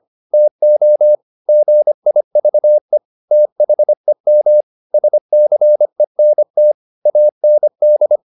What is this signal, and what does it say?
Morse code, heard as a keyed tone, about 25 wpm, 600 Hz. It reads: TO GIVE THEM SCENT AND
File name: the old man and the sea part 1 25wpm